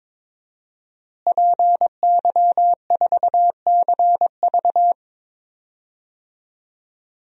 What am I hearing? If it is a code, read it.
Morse code, heard as a keyed tone, about 22 wpm, 700 Hz. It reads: PY4CV
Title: DXCC-100 Call Signs - 5 Characters - Repeated 2x - 22wpm